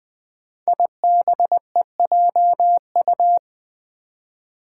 Morse code, heard as a keyed tone, about 20 wpm, 700 Hz. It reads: IBEJU
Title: Five Challenging Letter-Number Combinations 20wpm